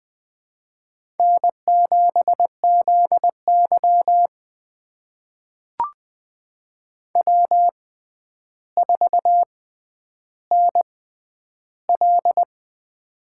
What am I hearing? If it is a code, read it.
Morse code, heard as a keyed tone, about 21 wpm, spaced as for 10 wpm, 700 Hz. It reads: N7ZY W4NL
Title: USA Call Signs - 4 Characters - 6x Character Spacing - 20wpm